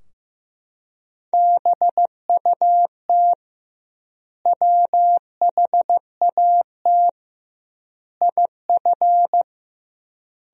Morse code, heard as a keyed tone, about 16 wpm, 700 Hz. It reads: BUT WHAT IF
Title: 2x Word-Spacing - Sentences from Top 100 Words - 15wpm